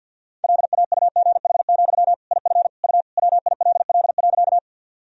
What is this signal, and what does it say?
Morse code, heard as a keyed tone, about 52 wpm, 700 Hz. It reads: LAUGH, IF U WILL.
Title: Sentences from Top 500 Words and Abbreviations 50wpm